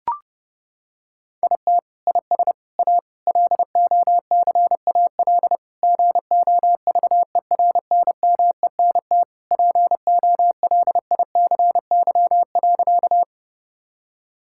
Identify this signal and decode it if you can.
Morse code, heard as a keyed tone, about 30 wpm, 700 Hz. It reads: IT IS A LOCAL GOVERNMENT POLICY.
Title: Sentences from Top 1000 Words - Review Part 5 - 30wpm